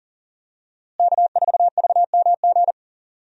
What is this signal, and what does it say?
Morse code, heard as a keyed tone, about 40 wpm, 700 Hz. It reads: K4VMG
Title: USA Call Signs - 5 Characters - Repeated 2x - 40wpm